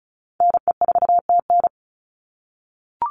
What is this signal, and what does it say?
Morse code, heard as a keyed tone, about 34 wpm, 700 Hz. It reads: NE4TN
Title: NA SOTA Chaser Callsigns 35wpm